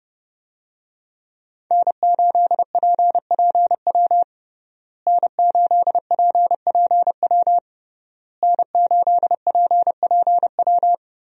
Morse code, heard as a keyed tone, about 30 wpm, 700 Hz. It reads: N8PPW N8PPW N8PPW
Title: USA Call Signs - 5 Characters - Repeated 3x - 30wpm